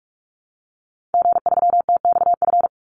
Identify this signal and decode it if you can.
Morse code, heard as a keyed tone, about 44 wpm, 700 Hz. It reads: G3TXF
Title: Call Signs Repeated 3x - 45wpm